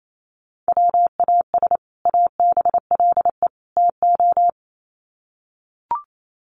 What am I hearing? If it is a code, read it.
Morse code, heard as a keyed tone, about 28 wpm, 700 Hz. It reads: WAS ABLE TO
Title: Sets of 3 Words 28wpm